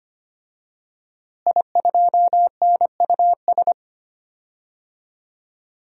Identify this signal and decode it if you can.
Morse code, heard as a keyed tone, about 25 wpm, 700 Hz. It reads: I2NUS
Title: DXCC-100 Call Signs - 5 Characters - Repeated 2x - 25wpm